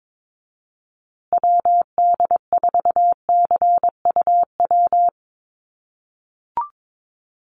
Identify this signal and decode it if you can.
Morse code, heard as a keyed tone, about 22 wpm, 700 Hz. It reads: WD4CUW
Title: USA Call Signs Repeated 3x 22wpm